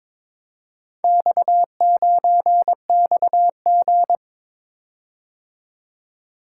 Morse code, heard as a keyed tone, about 22 wpm, 700 Hz. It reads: X9XG
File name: practice / 22wpm / Four Letter-Number Combinations - 22wpm